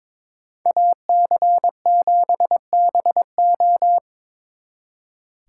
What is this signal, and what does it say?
Morse code, heard as a keyed tone, about 25 wpm, 700 Hz. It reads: AC7BO
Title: Speed Racing - USA 2x2 Call Signs 22wpm